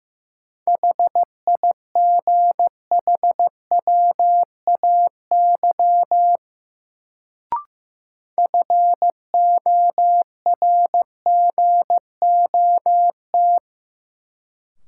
Morse code, heard as a keyed tone, about 15 wpm, 700 Hz. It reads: HIGHWAY FORGOT
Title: Sentences from Top 1400 Words - New Word Drill - 15wpm